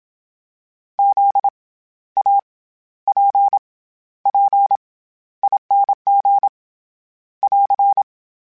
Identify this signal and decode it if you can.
Morse code, heard as a keyed tone, about 27 wpm, 800 Hz. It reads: Z A P P ING <AR>
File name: Special INGs-mp3s / Special INGs-mp3s / SlowToFaster ing @25